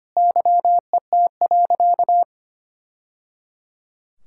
Morse code, heard as a keyed tone, about 25 wpm, 700 Hz. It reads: YET.
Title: Sentences from Top 1000 Words - Review Part 6 - 25wpm